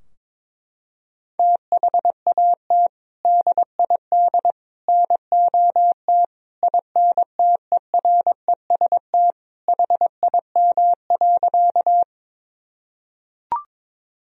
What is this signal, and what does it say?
Morse code, heard as a keyed tone, about 22 wpm, 700 Hz. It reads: THAT DID NOT INTEREST HIM.
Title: Sentences from Top 1000 Words - Review Part 2 - 22wpm